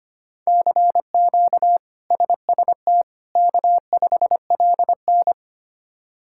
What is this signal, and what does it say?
Morse code, heard as a keyed tone, about 25 wpm, 700 Hz. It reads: CQ SST K5LN
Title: K1USN SST QSO Practice - 25wpm